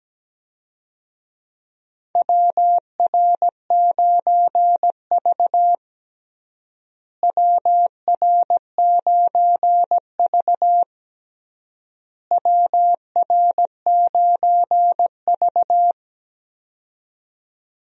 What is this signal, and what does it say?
Morse code, heard as a keyed tone, about 17 wpm, 700 Hz. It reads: WR9V WR9V WR9V
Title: USA Call Signs - 4 Characters - Repeated 3x - 17wpm